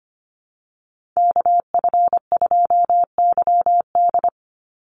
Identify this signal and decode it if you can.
Morse code, heard as a keyed tone, about 25 wpm, 700 Hz. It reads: KF2YD
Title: USA Call Signs 25wpm